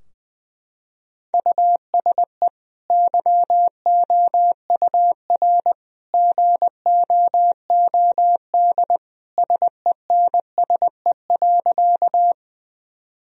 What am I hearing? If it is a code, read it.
Morse code, heard as a keyed tone, about 20 wpm, 700 Hz. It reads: USE YOUR GOOD SENSE.
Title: Sentences from Top 1000 Words - Review Part 4 - 20wpm